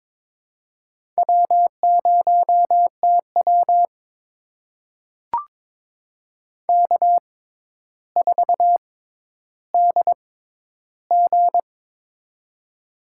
Morse code, heard as a keyed tone, about 21 wpm, spaced as for 11 wpm, 700 Hz. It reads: W0TW K4DG
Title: USA Call Signs - 6x Character Spacing - 22wpm